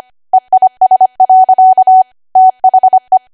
Morse code, heard as a keyed tone, about 25 wpm, 750 Hz. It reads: EIS. THE
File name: Archive / 25 WPM / 25WPM